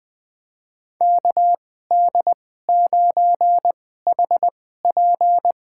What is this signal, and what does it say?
Morse code, heard as a keyed tone, about 20 wpm, spaced as for 14 wpm, 700 Hz. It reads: KD9HP
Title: USA Call Signs - 2x Character Spacing - 20wpm